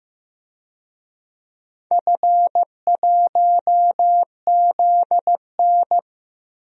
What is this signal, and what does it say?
Morse code, heard as a keyed tone, about 15 wpm, 700 Hz. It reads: F1ZN
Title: DXCC-100 Call Signs - 4 Characters - 15wpm